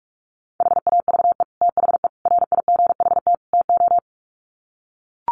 Morse code, heard as a keyed tone, about 44 wpm, 700 Hz. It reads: HAVE THE RIGHT TO
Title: Sets of 4 Words 45wpm